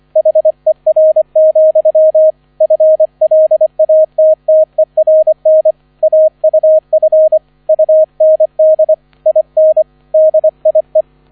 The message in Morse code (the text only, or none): HER, FLATTERN AUF UND IN DIE